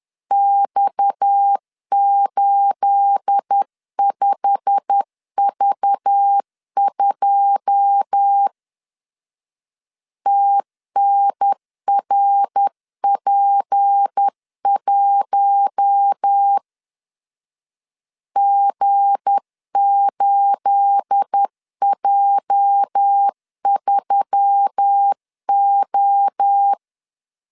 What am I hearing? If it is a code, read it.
Morse code, heard as a keyed tone, about 10 wpm, 800 Hz. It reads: X85V2 TNRP1 G8J3O
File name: LEZIONI / 9 28 10wpm 2842